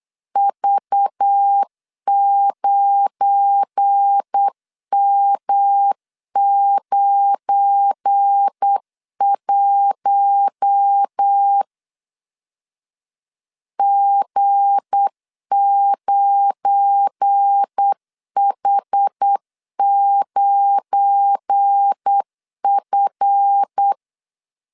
Morse code, heard as a keyed tone, about 8 wpm, 800 Hz. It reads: V9M91 G9H9F
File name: LEZIONI / 7 GF9Z / 7 GF9 8wpm 3545